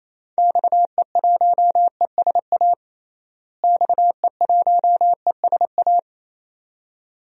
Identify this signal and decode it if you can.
Morse code, heard as a keyed tone, about 28 wpm, 700 Hz. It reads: XE1ESA XE1ESA
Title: DXCC-100 Call Signs - 6 Characters - Repeated 2x - 28wpm